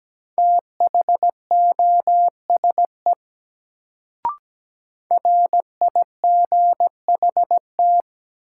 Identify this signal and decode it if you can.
Morse code, heard as a keyed tone, about 17 wpm, 700 Hz. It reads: THOSE RIGHT
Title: Sets of 2 Words - New Word Drill - 17wpm